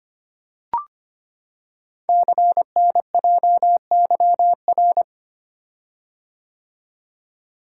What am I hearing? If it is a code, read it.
Morse code, heard as a keyed tone, about 25 wpm, 700 Hz. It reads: CNJYR
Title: Code Groups - 5 Letters - 25wpm